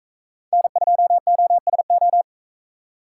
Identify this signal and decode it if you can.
Morse code, heard as a keyed tone, about 43 wpm, 700 Hz. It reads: N2OSO